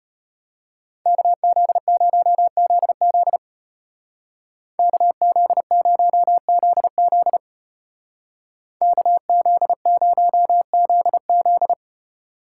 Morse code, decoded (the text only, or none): KZ0ZZ KZ0ZZ KZ0ZZ